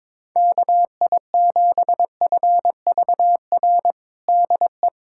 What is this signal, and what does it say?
Morse code, heard as a keyed tone, about 22 wpm, 700 Hz. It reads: KI7FVR DE
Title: USA Callsign DE Callsign 22wpm